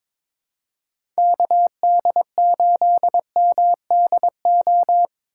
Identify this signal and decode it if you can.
Morse code, heard as a keyed tone, about 22 wpm, 700 Hz. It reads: KD8MDO
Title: USA Call Signs - 6 Characters - 2x Character Spacing - 22wpm